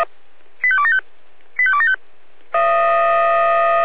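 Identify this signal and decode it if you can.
Five-tone selective call (ZVEI), heard as quick runs of five tones, tones of about 70 ms. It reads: 85275 85275
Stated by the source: ZVEI five-tone selective call received by SDR.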